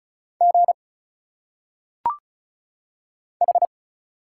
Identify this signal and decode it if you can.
Morse code, heard as a keyed tone, about 36 wpm, 700 Hz. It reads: G H
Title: Single Letters - Phonetic Pronunciation 35wpm